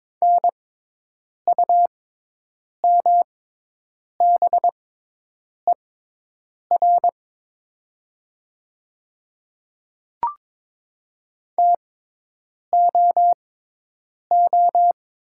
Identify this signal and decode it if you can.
Morse code, heard as a keyed tone, about 22 wpm, spaced as for 8 wpm, 700 Hz. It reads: NUMBER TOO
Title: 6x Character Spacing - Top 200 Words - 22wpm